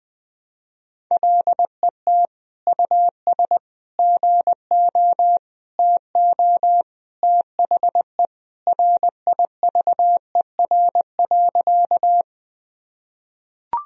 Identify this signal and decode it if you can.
Morse code, heard as a keyed tone, about 20 wpm, 700 Hz. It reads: LET US GO TO THE RIVER.
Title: Sentences from Top 1000 Words - Review Part 1 - 20wpm